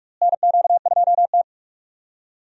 Morse code, heard as a keyed tone, about 46 wpm, 700 Hz. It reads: NQ2T
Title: USA Call Signs - 4 Characters - Repeated 3x - 45wpm